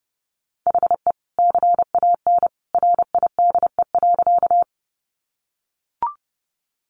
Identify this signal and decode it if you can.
Morse code, heard as a keyed tone, about 30 wpm, 700 Hz. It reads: HE CAN RIDE.
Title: Sentences from Top 1200 Words 30wpm